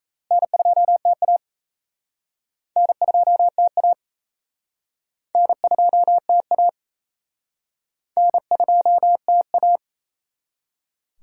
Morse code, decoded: N2TA N2TA N2TA N2TA

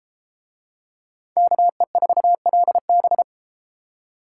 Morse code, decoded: KE4LB

